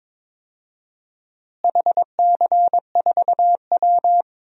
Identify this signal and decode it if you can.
Morse code, heard as a keyed tone, about 22 wpm, 700 Hz. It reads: HC4W
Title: DXCC-100 Call Signs - 4 Characters - Repeated 3x - 22wpm